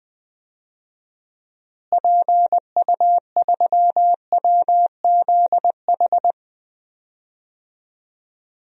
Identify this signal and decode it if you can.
Morse code, heard as a keyed tone, about 21 wpm, 700 Hz. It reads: PU3WZH
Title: DXCC-100 Call Signs - 6 Characters - Repeated 2x - 20wpm